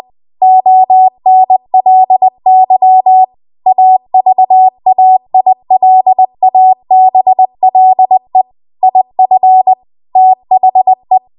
Morse code, heard as a keyed tone, about 20 wpm, 750 Hz. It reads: ONLY AVAILABLE IF THE
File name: Archive / 20 WPM / 20WPM